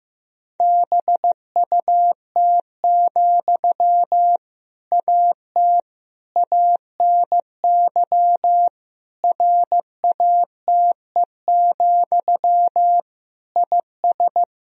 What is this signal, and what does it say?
Morse code, heard as a keyed tone, about 15 wpm, 700 Hz. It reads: BUT, AT ANY RATE, IS
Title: Sentences from Top 1000 Words - Review Part 4 - 15wpm